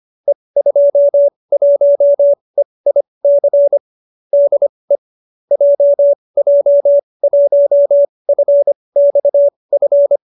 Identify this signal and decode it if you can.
Morse code, heard as a keyed tone, about 25 wpm, spaced as for 20 wpm, 570 Hz. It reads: E21EIC DE JJ1FXF